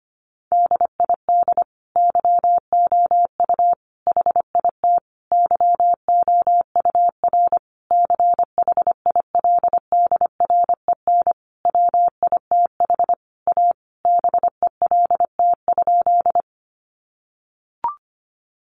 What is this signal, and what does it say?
Morse code, heard as a keyed tone, about 25 wpm, 700 Hz. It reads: DID YOU HIT YOUR CHILDREN WITH A BELT?
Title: Sentences from Top 1400 Words 25wpm